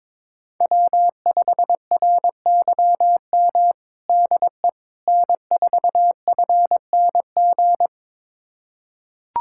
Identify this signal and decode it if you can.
Morse code, heard as a keyed tone, about 22 wpm, 700 Hz. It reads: W5RYM DE N4FNG
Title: USA Callsign DE Callsign - 5 Characters - 22wpm